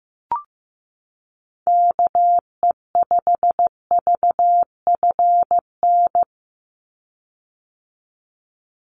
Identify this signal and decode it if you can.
Morse code, heard as a keyed tone, about 15 wpm, 700 Hz. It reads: KE5VFN
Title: USA Call Signs 15wpm